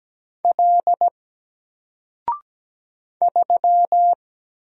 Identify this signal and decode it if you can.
Morse code, heard as a keyed tone, about 17 wpm, 700 Hz. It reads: L 3
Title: Single Character (Letter:Number:Punctuation:Prosign) Alt 17wpm